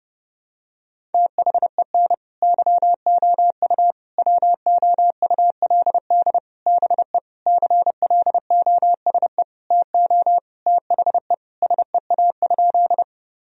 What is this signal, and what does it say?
Morse code, heard as a keyed tone, about 30 wpm, 700 Hz. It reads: THEN YOU WOULD BE CLOSE TO THE SEA?
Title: Sentences from Top 1000 Words - Review Part 1 - 30wpm